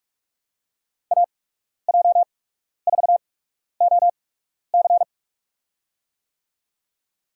Morse code, heard as a keyed tone, about 41 wpm, spaced as for 12 wpm, 700 Hz. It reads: AJ4OC
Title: USA Call Signs - 5 Characters - 8x Character Spacing - 45wpm